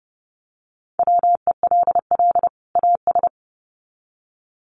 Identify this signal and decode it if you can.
Morse code, heard as a keyed tone, about 30 wpm, 700 Hz. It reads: WELL AS
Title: Sets of 2 Words 30wpm